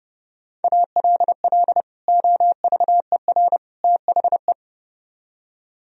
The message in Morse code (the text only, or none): ALL OVER THE